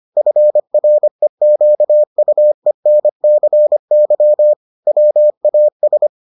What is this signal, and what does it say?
Morse code, heard as a keyed tone, about 25 wpm, 600 Hz. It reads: FREQUENCY WAS